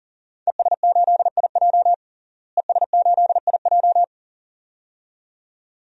Morse code, heard as a keyed tone, about 40 wpm, 700 Hz. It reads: ES8IJ ES8IJ
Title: DXCC-100 Call Signs - 5 Characters - Repeated 2x - 40wpm